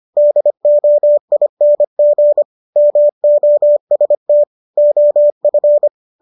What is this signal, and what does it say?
Morse code, heard as a keyed tone, about 25 wpm, 590 Hz. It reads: DOING MOST OF